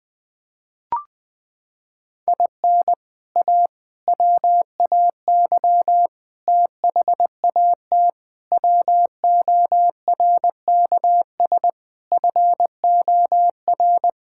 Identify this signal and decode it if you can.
Morse code, heard as a keyed tone, about 20 wpm, 700 Hz. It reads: IN A WAY THAT WORKS FOR
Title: Sentences with 4-Word Phrases - 20wpm